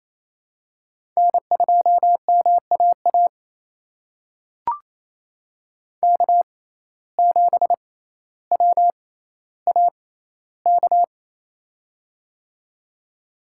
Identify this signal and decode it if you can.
Morse code, heard as a keyed tone, about 28 wpm, spaced as for 14 wpm, 700 Hz. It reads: N2MAA K7WAK
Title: USA Call Signs - 5 Characters - 6x Character Spacing - 28wpm